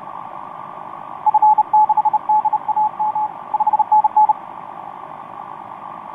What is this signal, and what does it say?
Morse code, heard as a keyed tone, about 32 wpm, 860 Hz. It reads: F6DAM 5NN
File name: t46rrc i2uua etc 12m cw 18feb2013 1528